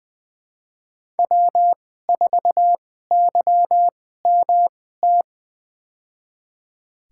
Morse code, heard as a keyed tone, about 20 wpm, spaced as for 14 wpm, 700 Hz. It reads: W4YMT
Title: USA Call Signs - 5 Characters - 2x Character Spacing - 20wpm